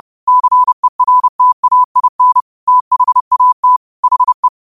Morse code, heard as a keyed tone, about 30 wpm, 1000 Hz. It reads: CERTAIN THAT HE